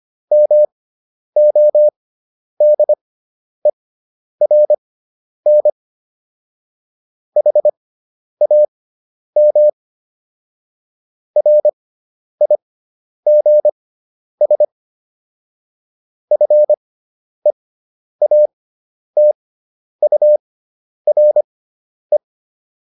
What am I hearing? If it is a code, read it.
Morse code, heard as a keyed tone, about 25 wpm, spaced as for 10 wpm, 600 Hz. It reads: MODERN HAM RIGS FEATURE